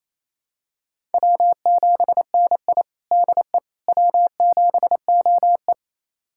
Morse code, heard as a keyed tone, about 28 wpm, 700 Hz. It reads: W7NI DE W7OE
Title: USA Callsign DE Callsign - 4 Characters - 28wpm